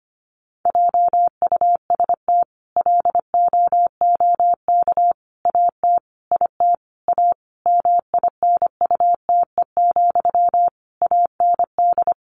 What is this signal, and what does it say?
Morse code, heard as a keyed tone, about 25 wpm, 700 Hz. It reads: JUST LOOK AT IT A MINUTE, AND
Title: Sentences from Top 1000 Words 25wpm